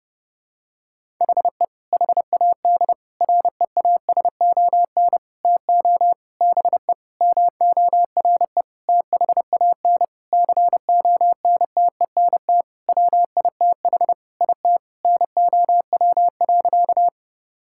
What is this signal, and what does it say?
Morse code, heard as a keyed tone, about 30 wpm, 700 Hz. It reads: HE HAD REASON TO BE MORE THAN CONTENT WITH IT NOW.
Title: Sentences from Top 1000 Words - Review Part 6 - 30wpm